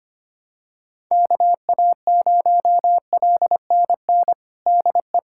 Morse code, heard as a keyed tone, about 25 wpm, 700 Hz. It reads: KA0LNN DE
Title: USA Callsign DE Callsign - 6 Characters - 25wpm